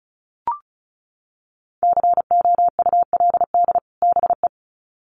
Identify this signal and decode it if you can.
Morse code, heard as a keyed tone, about 34 wpm, 700 Hz. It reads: COULD BE
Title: Sets of 2 Words 35wpm